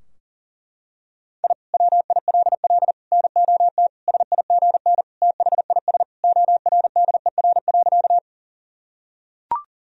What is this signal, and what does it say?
Morse code, heard as a keyed tone, about 37 wpm, 700 Hz. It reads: I WILL NOT SIGN THIS ORDER.
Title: Sentences from Top 1000 Words - Review Part 2 - 40wpm